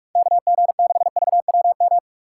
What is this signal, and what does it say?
Morse code, heard as a keyed tone, about 46 wpm, 700 Hz. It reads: KG6VWM
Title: USA Call Signs - 6 Characters - 2x Character Spacing - 45wpm